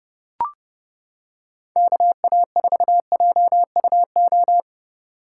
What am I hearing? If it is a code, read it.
Morse code, heard as a keyed tone, about 29 wpm, 700 Hz. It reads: KA4JUO